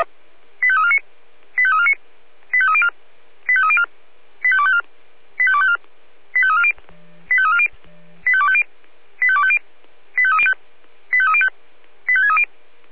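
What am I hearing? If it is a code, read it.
Five-tone selective call (ZVEI), heard as quick runs of five tones, tones of about 70 ms. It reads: 85349 85349 85304 85304 85254 85254 85340 85340 85250 85250 85305 85305 85630
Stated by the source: ZVEI five-tone selective call 85350 received by SDR.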